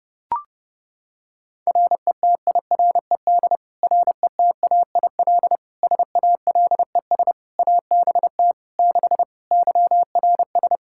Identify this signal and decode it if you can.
Morse code, heard as a keyed tone, about 32 wpm, 700 Hz. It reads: RETIRED RETAIL SALES ABT 6 YRS
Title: Practice QSO - Exchange 3 - More Info - 30wpm